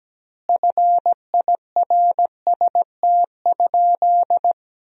Morse code, heard as a keyed tone, about 17 wpm, 700 Hz. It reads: FIRST?
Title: 2x Word-Spacing - Sentences from Top 100 Words - 17wpm